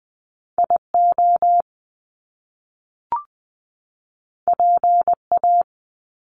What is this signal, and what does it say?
Morse code, heard as a keyed tone, about 20 wpm, 700 Hz. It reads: IO PA
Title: Top Two Letter Combinations in Words 20wpm